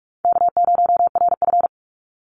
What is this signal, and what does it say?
Morse code, heard as a keyed tone, about 44 wpm, 700 Hz. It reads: K0RF
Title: Call Signs - 45wpm